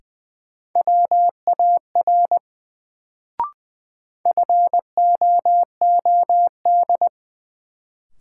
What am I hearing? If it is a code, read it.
Morse code, heard as a keyed tone, about 20 wpm, 700 Hz. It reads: WAR FOOD
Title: Sets of 5 Words - New Word Drill - 20wpm